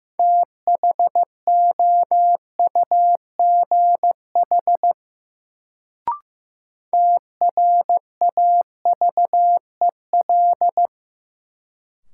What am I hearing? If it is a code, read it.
Morse code, heard as a keyed tone, about 15 wpm, 700 Hz. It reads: THOUGH TRAVEL
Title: Sentences from Top 400 Words - New Word Drill - 15wpm